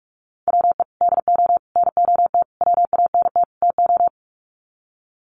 Morse code, heard as a keyed tone, about 44 wpm, 700 Hz. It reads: WE DO NOT WANT TO
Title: Sets of 5 Words 45wpm